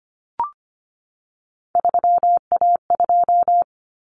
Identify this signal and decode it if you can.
Morse code, heard as a keyed tone, about 25 wpm, 700 Hz. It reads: 3A2
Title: Common Cut Numbers - 3-Digit 25wpm